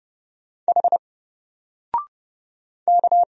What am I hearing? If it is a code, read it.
Morse code, heard as a keyed tone, about 30 wpm, 700 Hz. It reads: H K
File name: Single-Letter-Number-30wpm